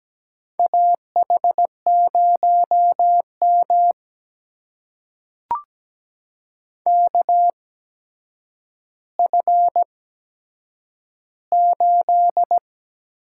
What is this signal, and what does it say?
Morse code, heard as a keyed tone, about 17 wpm, spaced as for 8 wpm, 700 Hz. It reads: AH0M KF8
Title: USA Call Signs - 4 Characters - 8x Character Spacing - 17wpm